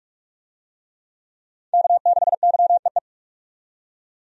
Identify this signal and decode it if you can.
Morse code, heard as a keyed tone, about 42 wpm, 700 Hz. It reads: K6YEE
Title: USA Call Signs - 5 Characters - 45wpm